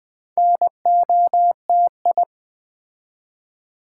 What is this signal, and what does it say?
Morse code, heard as a keyed tone, about 20 wpm, 700 Hz. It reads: NOTI
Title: Build A Word - 2 Letters At A Time - 6-Letter Words - 20wpm